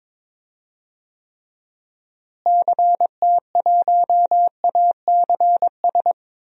Morse code, heard as a keyed tone, about 22 wpm, 700 Hz. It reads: CT1ACS
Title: DXCC-100 Call Signs - 6 Characters - Repeated 2x - 22wpm